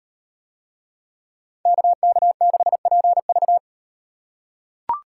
Morse code, heard as a keyed tone, about 36 wpm, 700 Hz. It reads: KK6PV